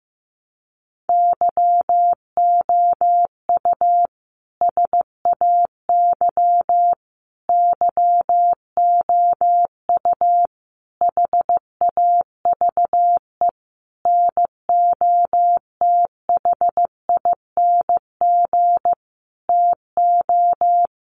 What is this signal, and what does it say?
Morse code, heard as a keyed tone, about 14 wpm, 700 Hz. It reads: YOU SAY YOU HAVE NOTHING TO
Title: Sentences from Top 700 Words 15wpm